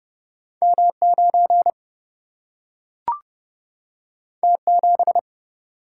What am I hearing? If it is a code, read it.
Morse code, heard as a keyed tone, about 31 wpm, 700 Hz. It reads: M9 T7
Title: Single-Letter Single-Number 30wpm